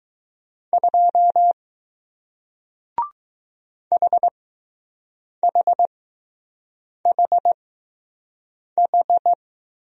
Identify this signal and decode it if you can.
Morse code, heard as a keyed tone, about 20 wpm, spaced as for 6 wpm, 700 Hz. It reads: 2 HHHH